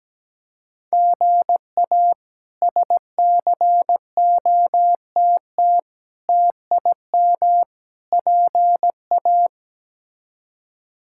Morse code, heard as a keyed tone, about 17 wpm, 700 Hz. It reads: GA SCOTT TIM PA
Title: K1USN SST QSO Practice - 17wpm